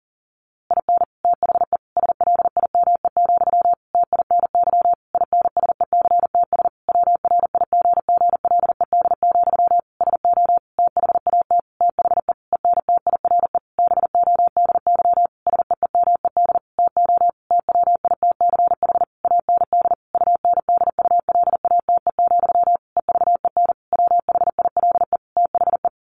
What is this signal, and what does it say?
Morse code, heard as a keyed tone, about 40 wpm, 700 Hz. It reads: IN THE SLIME, TINY INSECTS WRIGGLED, SO THAT THE ENTIRE BODY SEEMED TO TWITCH AND UNDULATE, EVEN WHILE THE
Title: A Sound Of Thunder 40wpm